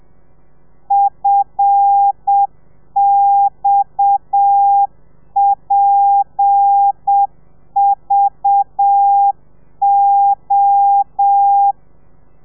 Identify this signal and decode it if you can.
Morse code, heard as a keyed tone, about 7 wpm, 800 Hz. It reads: FXPVO
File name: telegrafia / i2tfj / Lez7 30